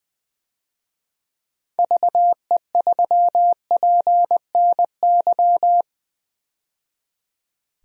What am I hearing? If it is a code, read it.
Morse code, heard as a keyed tone, about 20 wpm, 700 Hz. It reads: VE3PNY
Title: DXCC-100 Call Signs - 6 Characters - 20wpm